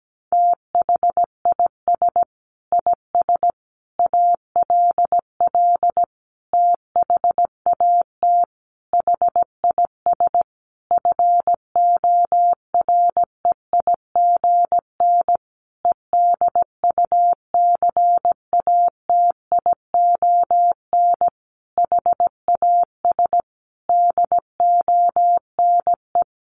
Morse code, heard as a keyed tone, about 17 wpm, 700 Hz. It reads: THIS IS ALL THAT HIS FOREIGN EDUCATION HAS DONE
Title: Sentences from Top 1400 Words 17wpm